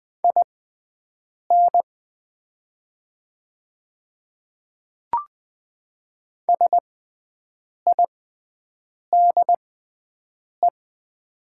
Morse code, heard as a keyed tone, about 20 wpm, spaced as for 7 wpm, 700 Hz. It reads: IN SIDE
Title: 6x Character Spacing - Top 100 Words - 20wpm